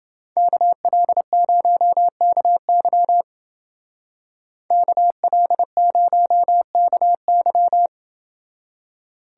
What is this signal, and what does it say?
Morse code, heard as a keyed tone, about 28 wpm, 700 Hz. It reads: KL0KY KL0KY